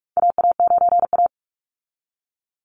Morse code, heard as a keyed tone, about 46 wpm, 700 Hz. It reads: AA9A
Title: Call Signs - 45wpm